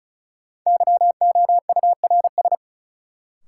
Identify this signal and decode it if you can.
Morse code, heard as a keyed tone, about 34 wpm, 700 Hz. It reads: YOURS
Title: Sentences from Top 1500 Words - New Word Drill - 35wpm